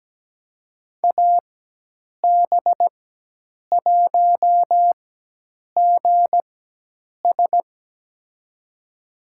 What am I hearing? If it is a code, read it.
Morse code, heard as a keyed tone, about 17 wpm, spaced as for 8 wpm, 700 Hz. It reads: AB1GS
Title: USA Call Signs - 5 Characters - 4x Character Spacing - 17wpm